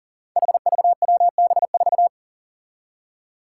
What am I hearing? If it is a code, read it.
Morse code, heard as a keyed tone, about 37 wpm, 700 Hz. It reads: HVWB4
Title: Five Challenging Letter-Number Combinations 40wpm